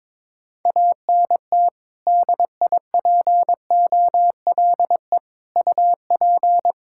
Morse code, heard as a keyed tone, about 22 wpm, 700 Hz. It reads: ANT DIPOLE UP
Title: Practice QSO - Exchange 2 - Info - 22wpm